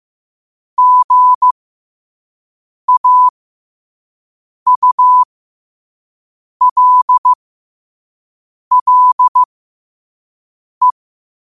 Morse code, heard as a keyed tone, about 15 wpm, spaced as for 5 wpm, 1000 Hz. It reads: GAULLE
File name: qotdfiles / 5wpm